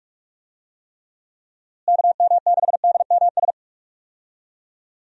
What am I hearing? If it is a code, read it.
Morse code, heard as a keyed tone, about 46 wpm, 700 Hz. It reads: KM6DMS
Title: USA Call Signs - 6 Characters - Repeated 3x - 45wpm